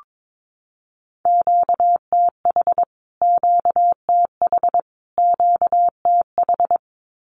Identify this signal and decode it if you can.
Morse code, heard as a keyed tone, about 22 wpm, 700 Hz. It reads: QTH QTH QTH
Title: Q-Codes Repeated 3x 22wpm